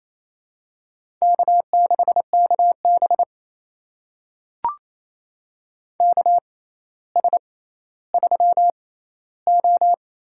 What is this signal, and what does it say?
Morse code, heard as a keyed tone, about 28 wpm, spaced as for 14 wpm, 700 Hz. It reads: K6KB KS3O
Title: USA Call Signs - 4 Characters - 6x Character Spacing - 28wpm